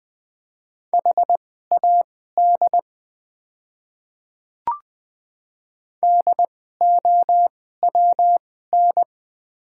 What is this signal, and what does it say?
Morse code, heard as a keyed tone, about 20 wpm, spaced as for 14 wpm, 700 Hz. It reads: HAD DOWN
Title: 2x Character Spacing - Top 100 Words - 20wpm